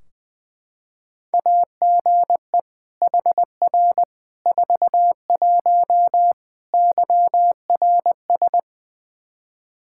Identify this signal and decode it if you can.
Morse code, heard as a keyed tone, about 21 wpm, 700 Hz. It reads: AGE HR 41 YRS
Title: Practice QSO - Exchange 3 - More Info - 20wpm